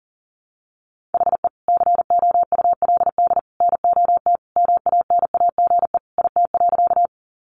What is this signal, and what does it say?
Morse code, heard as a keyed tone, about 40 wpm, 700 Hz. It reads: HE COULD NOT MANAGE IT.
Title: Sentences from Top 1400 Words 40wpm